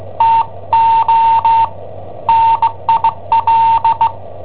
Morse code, heard as a keyed tone, about 17 wpm, 910 Hz. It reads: TO NIL
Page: Cassetta esercizi preparata da I7OHP per IW5ELP